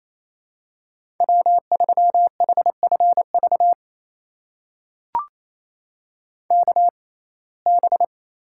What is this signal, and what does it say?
Morse code, heard as a keyed tone, about 27 wpm, spaced as for 20 wpm, 700 Hz. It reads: W3HFV KB
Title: USA Call Signs - 6x Character Spacing - 28wpm